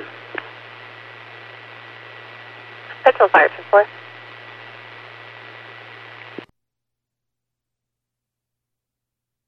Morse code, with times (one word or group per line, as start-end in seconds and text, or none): none